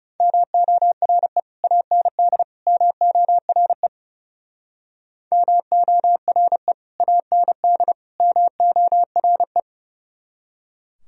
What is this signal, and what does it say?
Morse code, heard as a keyed tone, about 38 wpm, 700 Hz. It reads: MORE AND MORE MORE AND MORE
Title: Speed-Racing - Sets of 3 Words 30wpm